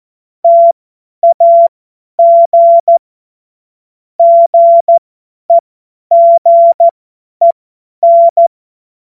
Morse code, heard as a keyed tone, about 14 wpm, spaced as for 10 wpm, 680 Hz. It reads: TAG GEGEN
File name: Morsekurs / Witze / tz 08. November 2022 50 F70